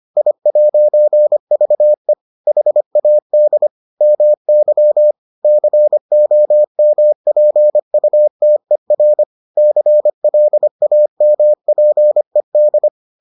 Morse code, heard as a keyed tone, about 25 wpm, 600 Hz. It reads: I'VE HAD MY COMPUTER CLAMPED